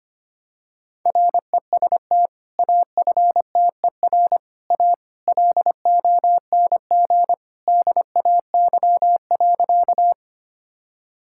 Morse code, heard as a keyed tone, about 25 wpm, 700 Hz. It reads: REST AFTER A LONG DAY.
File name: practice / 25wpm / Sentences with 2-Word Phrases - 25wpm